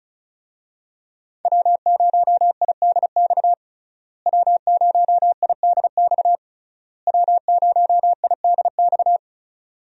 Morse code, heard as a keyed tone, about 36 wpm, 700 Hz. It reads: W0IDX W0IDX W0IDX
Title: USA Call Signs - 5 Characters - Repeated 3x - 35wpm